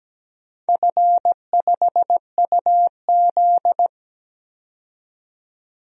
Morse code, heard as a keyed tone, about 17 wpm, 700 Hz. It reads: F5UZ
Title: DXCC-100 Call Signs - 4 Characters - 17wpm